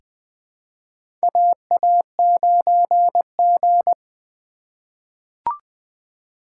Morse code, heard as a keyed tone, about 20 wpm, 700 Hz. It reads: AA9G
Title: USA Call Signs - 4 Characters - 2x Character Spacing - 20wpm